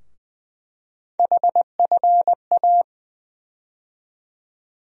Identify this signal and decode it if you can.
Morse code, heard as a keyed tone, about 20 wpm, 700 Hz. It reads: HFA
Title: USA Call Sign Postfixes - 3 Letters 20wpm